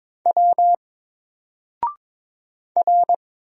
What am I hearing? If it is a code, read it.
Morse code, heard as a keyed tone, about 22 wpm, 700 Hz. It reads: W R
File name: Single-Letter-Number-22wpm